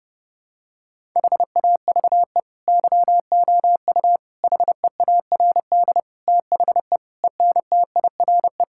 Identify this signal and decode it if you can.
Morse code, heard as a keyed tone, about 30 wpm, 700 Hz. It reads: HAVE YOU HEARD THE ENTIRE
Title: Sentences from Top 1000 Words - Review Part 5 - 30wpm